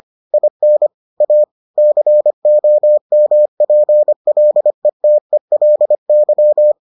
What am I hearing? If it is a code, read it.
Morse code, heard as a keyed tone, about 25 wpm, 600 Hz. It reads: IN A COMPLETELY